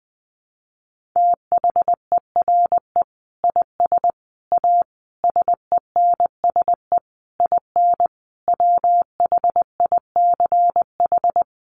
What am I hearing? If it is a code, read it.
Morse code, heard as a keyed tone, about 20 wpm, 700 Hz. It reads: THERE IS A SENSE IN WHICH
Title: Sentences from Top 1500 Words 20wpm